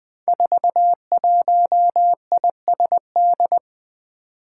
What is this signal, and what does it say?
Morse code, heard as a keyed tone, about 20 wpm, 700 Hz. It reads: 41ISD
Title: Five Challenging Letter-Number Combinations 20wpm